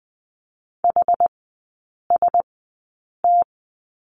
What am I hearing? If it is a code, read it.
Morse code, heard as a keyed tone, about 20 wpm, spaced as for 8 wpm, 700 Hz. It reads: HST
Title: 093 Lesson 16e - Letter c - Three Characters 20wpm